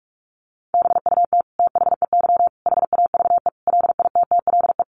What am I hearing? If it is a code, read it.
Morse code, heard as a keyed tone, about 44 wpm, 700 Hz. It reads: BUT THEY HAVE LITTLE
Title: Sentences from Top 1200 Words 45wpm